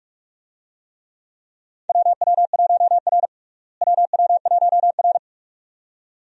Morse code, heard as a keyed tone, about 43 wpm, 700 Hz. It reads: WW1R WW1R
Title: USA Call Signs - 4 Characters - Repeated 2x - 45wpm